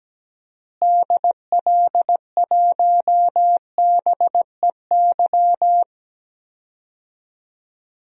Morse code, heard as a keyed tone, about 17 wpm, 700 Hz. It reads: DL1BEY
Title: DXCC-100 Call Signs - 6 Characters - Repeated 2x - 17wpm